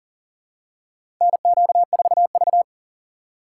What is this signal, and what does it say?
Morse code, heard as a keyed tone, about 40 wpm, 700 Hz. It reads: NQ4V
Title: USA Call Signs - 4 Characters - 3x Character Spacing - 40wpm